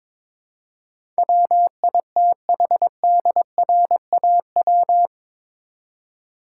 Morse code, heard as a keyed tone, about 22 wpm, 700 Hz. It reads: WITHDRAW
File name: practice / 22wpm / Eight Letter Words - 22wpm